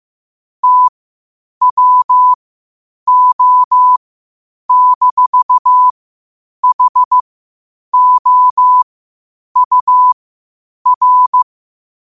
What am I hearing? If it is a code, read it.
Morse code, heard as a keyed tone, about 15 wpm, spaced as for 8 wpm, 1000 Hz. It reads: TWO-HOUR
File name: qotdfiles / 8wpm